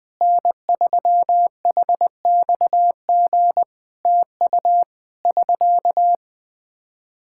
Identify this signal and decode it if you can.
Morse code, heard as a keyed tone, about 20 wpm, 700 Hz. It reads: N3HXG TU <SK>
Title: Practice QSO - Exchange 4 - Ending - 20wpm